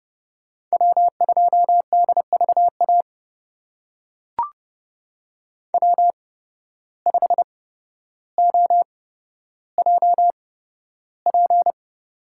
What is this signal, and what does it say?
Morse code, heard as a keyed tone, about 30 wpm, spaced as for 13 wpm, 700 Hz. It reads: W2DVA W5OJP
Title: USA Call Signs - 5 Characters - 8x Character Spacing - 30wpm